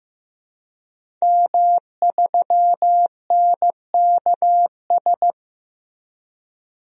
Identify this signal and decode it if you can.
Morse code, heard as a keyed tone, about 15 wpm, 700 Hz. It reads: M3NKS
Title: DXCC-100 Call Signs - 5 Characters - Repeated 3x - 15wpm